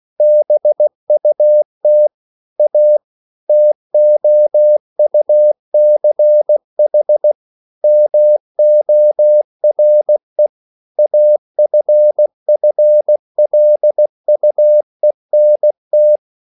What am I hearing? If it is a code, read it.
Morse code, heard as a keyed tone, about 16 wpm, 600 Hz. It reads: BUT A TOUCH MORE AFFLUENT